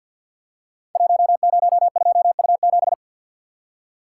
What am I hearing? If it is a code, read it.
Morse code, heard as a keyed tone, about 50 wpm, 700 Hz. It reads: 102U7
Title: Code Groups - 5 Letter:Numbers - 50wpm